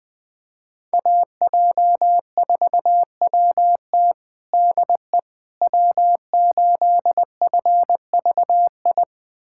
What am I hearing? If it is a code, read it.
Morse code, heard as a keyed tone, about 20 wpm, 700 Hz. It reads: AJ4WT DE W8FVI
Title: USA Callsign DE Callsign - 5 Characters - 20wpm